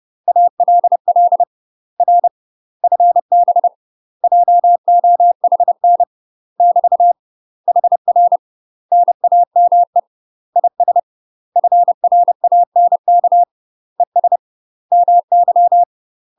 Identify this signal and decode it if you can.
Morse code, heard as a keyed tone, about 30 wpm, 700 Hz. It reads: ALL R FB JOHN <BT> HR NAME IS FRANK ES MY
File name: UK MASTER QSOs TRAINING / 30WPM / UK MASTER QSOs 0035